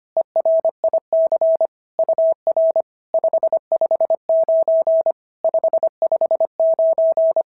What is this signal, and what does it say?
Morse code, heard as a keyed tone, about 25 wpm, 650 Hz. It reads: ERIC UR 559 559
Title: SOTA QSO Practice 25wpm